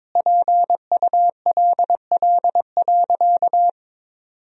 Morse code, heard as a keyed tone, about 22 wpm, 700 Hz. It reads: PULL.
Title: Sentences from Top 1000 Words - Review Part 2 - 22wpm